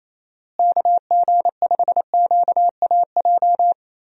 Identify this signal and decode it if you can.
Morse code, heard as a keyed tone, about 28 wpm, 700 Hz. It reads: KG5QAJ
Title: USA Call Signs - 6 Characters - Repeated 2x - 28wpm